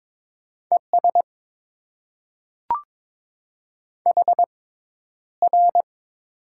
Morse code, heard as a keyed tone, about 22 wpm, spaced as for 11 wpm, 700 Hz. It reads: ES HR
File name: Top 90 QSO Elements - 6x Character Spacing - 22wpm